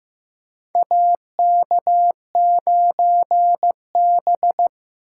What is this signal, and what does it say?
Morse code, heard as a keyed tone, about 15 wpm, 700 Hz. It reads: AK9B
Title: USA Call Signs - 4 Characters - 2x Character Spacing - 15wpm